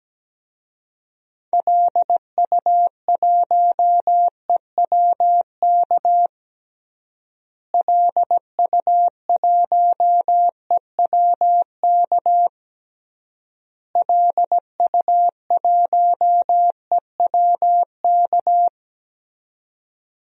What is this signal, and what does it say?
Morse code, heard as a keyed tone, about 17 wpm, 700 Hz. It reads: LU1EWK LU1EWK LU1EWK
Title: DXCC-100 Call Signs - 6 Characters - Repeated 3x - 17wpm